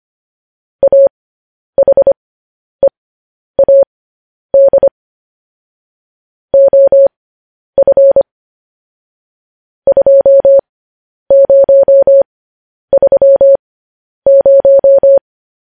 Morse code, heard as a keyed tone, about 25 wpm, spaced as for 10 wpm, 550 Hz. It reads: AHEAD OF 2030
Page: news headlines in morse code at 10 wpm